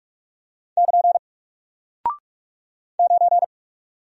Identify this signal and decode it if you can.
Morse code, heard as a keyed tone, about 44 wpm, 700 Hz. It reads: <KN> 9
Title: Single Character (Letter:Number:Punctuation:Prosign) Alt 45wpm